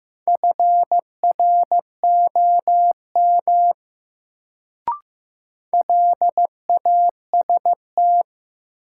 Morse code, heard as a keyed tone, about 15 wpm, 700 Hz. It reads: FROM LAST
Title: Sets of 2 Words - New Word Drill - 15wpm